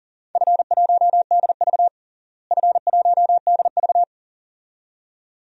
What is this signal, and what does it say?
Morse code, heard as a keyed tone, about 40 wpm, 700 Hz. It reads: F1DV F1DV
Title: DXCC-100 Call Signs - 4 Characters - Repeated 2x - 40wpm